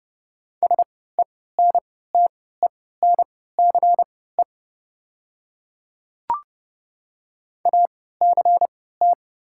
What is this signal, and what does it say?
Morse code, heard as a keyed tone, about 30 wpm, spaced as for 17 wpm, 700 Hz. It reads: SENTENCE ACT
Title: 3x Character Spacing - Top 300 Words - 30wpm